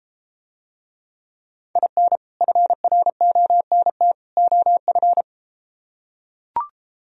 Morse code, heard as a keyed tone, about 33 wpm, 700 Hz. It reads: IN FRONT OF